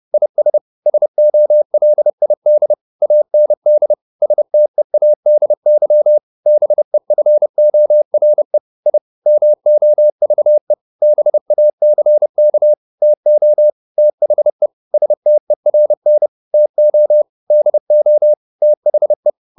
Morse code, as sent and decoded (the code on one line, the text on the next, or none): .. ... / ... --- .-.. .. -.. / .- -. -.. / ... - . .- -.. -.-- / -... ..-. --- .-. / .. / -- --- ...- . / -... .- -.-. -.- / - --- / - .... / ... - . .-. -. / - --- / -.. --- / - .... .
IS SOLID AND STEADY BEFORE I MOVE BACK TO THE STERN TO DO THE